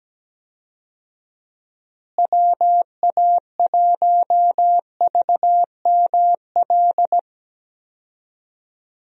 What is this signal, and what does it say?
Morse code, heard as a keyed tone, about 17 wpm, 700 Hz. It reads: WA1VML
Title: USA Call Signs - 6 Characters - Repeated 3x - 17wpm